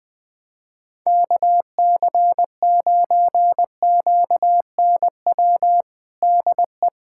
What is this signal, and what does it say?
Morse code, heard as a keyed tone, about 21 wpm, 700 Hz. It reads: KC9QNW DE
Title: USA Callsign DE Callsign 20wpm